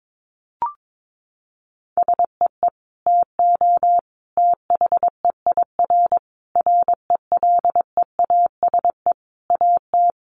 Morse code, heard as a keyed tone, about 22 wpm, 700 Hz. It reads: SEE TO THEIR RELEASE AT
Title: Sentences from Top 1100 Words 22wpm